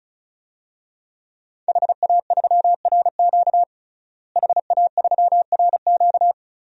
Morse code, heard as a keyed tone, about 36 wpm, 700 Hz. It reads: HA3RQ HA3RQ
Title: DXCC-100 Call Signs - 5 Characters - Repeated 2x - 35wpm